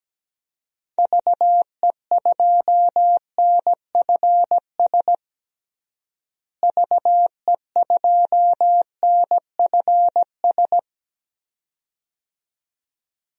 Morse code, heard as a keyed tone, about 17 wpm, 700 Hz. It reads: VE2NFS VE2NFS
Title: DXCC-100 Call Signs - 6 Characters - Repeated 2x - 17wpm